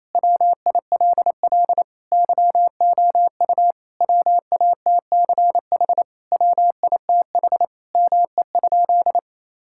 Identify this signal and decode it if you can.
Morse code, heard as a keyed tone, about 28 wpm, 700 Hz. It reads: WILL YOU WATCH WITH ME?
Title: Sentences from Top 1000 Words - Review Part 5 - 28wpm